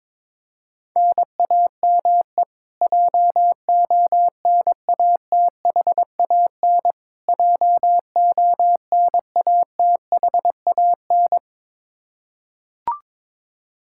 Morse code, heard as a keyed tone, about 22 wpm, 700 Hz. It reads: NAME JONATHAN JONATHAN
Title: Practice QSO - Exchange 1 - Beginning - 22wpm